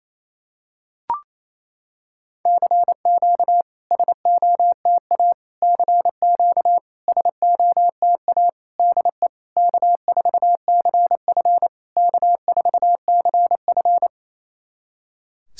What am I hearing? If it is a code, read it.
Morse code, heard as a keyed tone, about 29 wpm, 700 Hz. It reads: CQ SOTA CQ SOTA DE K4CF K4CF
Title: SOTA QSO Practice 28wpm